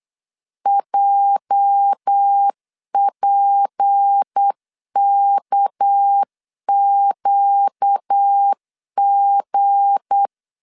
Morse code, heard as a keyed tone, about 8 wpm, 800 Hz. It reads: JPKQG
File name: CW / DET / 1 9 1500C S A 8wpm 5637